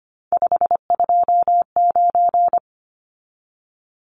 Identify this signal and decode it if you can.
Morse code, heard as a keyed tone, about 25 wpm, 700 Hz. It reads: 529
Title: Numbers - 3-Digit 25wpm